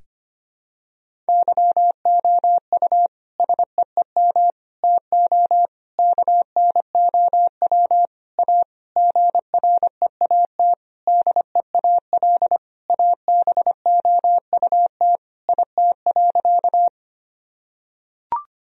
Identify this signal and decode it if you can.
Morse code, heard as a keyed tone, about 25 wpm, 700 Hz. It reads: YOU SEEM TO KNOW A GREAT DEAL ABOUT IT.
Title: Sentences from Top 1000 Words - Review Part 1 - 25wpm